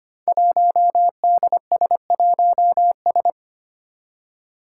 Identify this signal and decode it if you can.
Morse code, heard as a keyed tone, about 25 wpm, 700 Hz. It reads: 1DS1S
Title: Five Challenging Letter-Number Combinations 25wpm